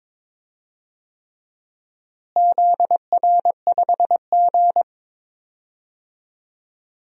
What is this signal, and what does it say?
Morse code, heard as a keyed tone, about 22 wpm, 700 Hz. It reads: ZR5G
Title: DXCC-100 Call Signs - 4 Characters - Repeated 3x - 22wpm